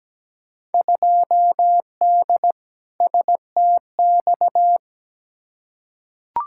Morse code, heard as a keyed tone, about 17 wpm, 700 Hz. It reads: 2D STX
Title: Field Day Exchanges - 17wpm